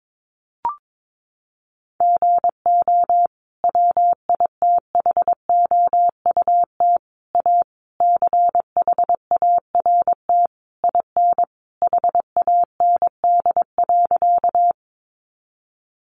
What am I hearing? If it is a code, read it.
Morse code, heard as a keyed tone, about 22 wpm, 700 Hz. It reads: GO WITHOUT A CHART IN HAND.
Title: Sentences from Top 1300 Words 22wpm